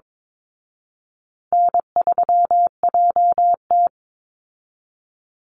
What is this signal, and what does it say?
Morse code, heard as a keyed tone, about 22 wpm, 700 Hz. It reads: N3JT
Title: Call Signs Repeated 2x - 22wpm